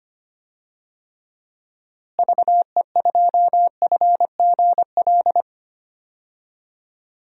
Morse code, heard as a keyed tone, about 25 wpm, 700 Hz. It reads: VE2FGL
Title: DXCC-100 Call Signs - 6 Characters - Repeated 3x - 25wpm